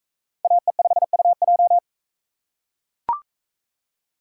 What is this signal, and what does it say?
Morse code, heard as a keyed tone, about 42 wpm, 700 Hz. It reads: AE5UJ